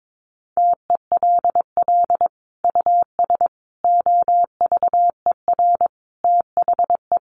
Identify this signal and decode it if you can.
Morse code, heard as a keyed tone, about 22 wpm, 700 Hz. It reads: TELL US OVER THE
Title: Sentences from Top 1100 Words 22wpm